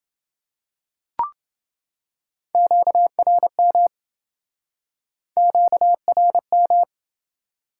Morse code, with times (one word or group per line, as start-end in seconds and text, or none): 2.5-3.9 QRM
5.4-6.8 QRM